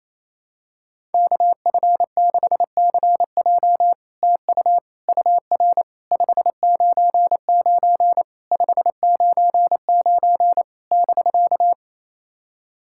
Morse code, heard as a keyed tone, about 27 wpm, 700 Hz. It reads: KF6CJ TU UR 599 599 <BK>
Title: SOTA QSO Practice 28wpm